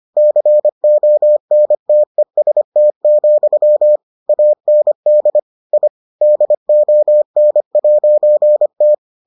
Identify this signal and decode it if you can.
Morse code, heard as a keyed tone, about 25 wpm, 600 Hz. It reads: CONTEST, AND I DON'T